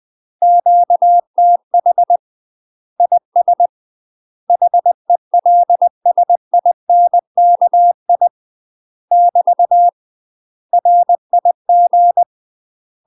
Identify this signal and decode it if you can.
Morse code, heard as a keyed tone, about 20 wpm, 700 Hz. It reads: QTH IS HELSINKI <BT> RIG
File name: UK MASTER QSOs TRAINING / 20WPM / UK MASTER QSOs 0011